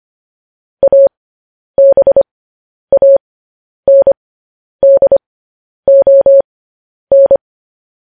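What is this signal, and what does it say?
Morse code, heard as a keyed tone, about 25 wpm, spaced as for 10 wpm, 550 Hz. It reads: ABANDON
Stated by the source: News headlines sent in Morse at 10 wpm.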